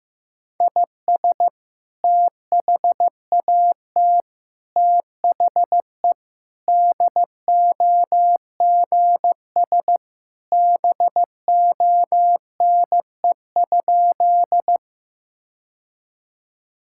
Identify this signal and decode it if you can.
Morse code, heard as a keyed tone, about 16 wpm, 700 Hz. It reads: IS THAT THE DOGS BONE?
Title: Sentences from Top 1600 Words 15wpm